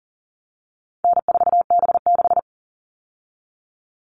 Morse code, heard as a keyed tone, about 40 wpm, 700 Hz. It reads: N4B6
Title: All Cut Numbers - 4-Digit 40wpm